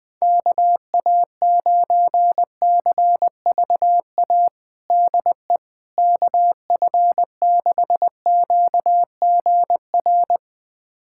Morse code, heard as a keyed tone, about 20 wpm, 700 Hz. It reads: KA9CVA DE KF6QGR
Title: USA Callsign DE Callsign 20wpm